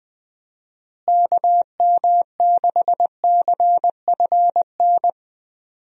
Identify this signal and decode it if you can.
Morse code, heard as a keyed tone, about 20 wpm, 700 Hz. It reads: KM6CFN
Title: USA Call Signs - 6 Characters - Repeated 2x - 20wpm